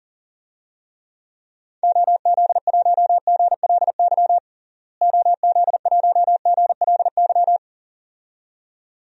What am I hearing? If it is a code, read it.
Morse code, heard as a keyed tone, about 40 wpm, 700 Hz. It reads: OZ1GLY OZ1GLY
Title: DXCC-100 Call Signs - 6 Characters - Repeated 2x - 40wpm